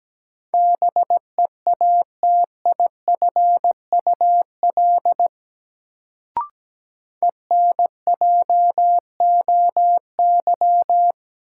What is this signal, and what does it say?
Morse code, heard as a keyed tone, about 17 wpm, 700 Hz. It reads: BEATIFUL ENJOY
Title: Sentences from Top 1300 Words - New Word Drill Alt - 17wpm